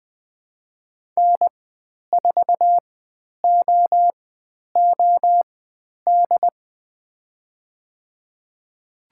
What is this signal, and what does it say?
Morse code, heard as a keyed tone, about 20 wpm, spaced as for 10 wpm, 700 Hz. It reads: N4OOD
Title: 102 Lesson 17g - Letter u - Call Signs 20wpm